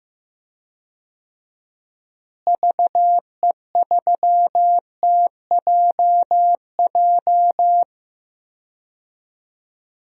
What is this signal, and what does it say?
Morse code, heard as a keyed tone, about 15 wpm, 700 Hz. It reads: VE3TJJ
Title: DXCC-100 Call Signs - 6 Characters - Repeated 3x - 15wpm